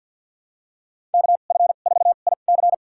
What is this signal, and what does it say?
Morse code, heard as a keyed tone, about 53 wpm, spaced as for 37 wpm, 700 Hz. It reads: KF4IC
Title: USA Call Signs - 5 Characters - 2x Character Spacing - 50wpm